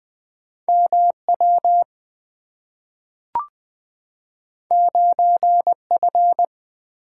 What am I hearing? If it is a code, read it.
Morse code, heard as a keyed tone, about 20 wpm, 700 Hz. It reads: M W 9 F
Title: Two Letter-Number Combinations - 20wpm